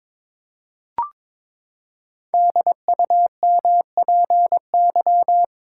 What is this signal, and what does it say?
Morse code, heard as a keyed tone, about 22 wpm, 700 Hz. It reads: DUMPY
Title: Pangram Challenge v2 - 22wpm